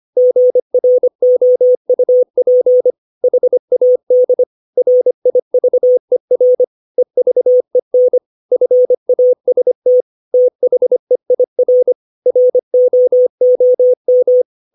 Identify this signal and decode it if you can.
Morse code, heard as a keyed tone, about 25 wpm, 500 Hz. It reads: GROUP HAD RIVER EVEN FAST THEIR ROOM
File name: 632 words / 632 Words 25WPM